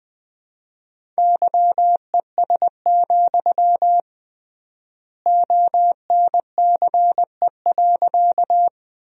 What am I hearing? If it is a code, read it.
Morse code, heard as a keyed tone, about 20 wpm, 700 Hz. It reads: YES, ONCE.
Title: 3x Word-Spacing - Sentences from Top 500 Words - 20wpm